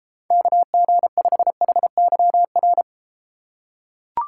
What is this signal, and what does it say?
Morse code, heard as a keyed tone, about 32 wpm, 700 Hz. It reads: KG5HYR